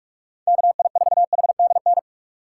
Morse code, heard as a keyed tone, about 46 wpm, 700 Hz. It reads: KI4HDN
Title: USA Call Signs - 6 Characters - 4x Character Spacing - 45wpm